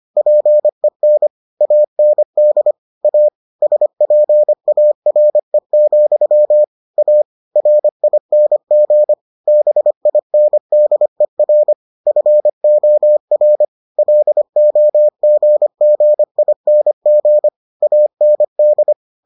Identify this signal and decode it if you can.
Morse code, heard as a keyed tone, about 25 wpm, 600 Hz. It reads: PEN AND A SPARE, A RING BINDER FOR LOGGING AND